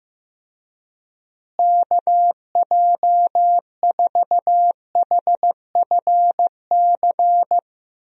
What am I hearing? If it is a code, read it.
Morse code, heard as a keyed tone, about 15 wpm, 700 Hz. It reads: KJ4HFC
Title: USA Call Signs - 6 Characters - Repeated 3x - 15wpm